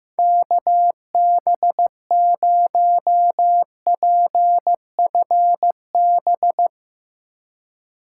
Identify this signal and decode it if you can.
Morse code, heard as a keyed tone, about 15 wpm, 700 Hz. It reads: KB0PFB
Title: Speed Racing - USA Call Signs 15wpm